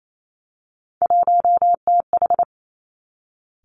Morse code, heard as a keyed tone, about 28 wpm, 700 Hz. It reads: 1TH
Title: Three Challenging Letter-Number Combinations 28wpm